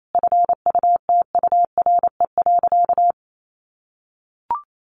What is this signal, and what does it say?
Morse code, heard as a keyed tone, about 28 wpm, 700 Hz. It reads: FUTURE.
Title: Sentences from Top 1200 Words 28wpm